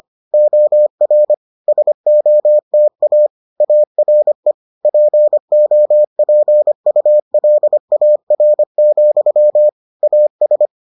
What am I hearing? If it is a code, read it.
Morse code, heard as a keyed tone, about 25 wpm, 600 Hz. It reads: OR SOTA ARE POPULAR, AS